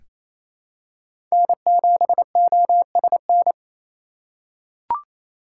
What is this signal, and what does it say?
Morse code, heard as a keyed tone, about 28 wpm, 700 Hz. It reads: N7OSN
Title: USA Call Signs - 5 Characters - 4x Character Spacing - 28wpm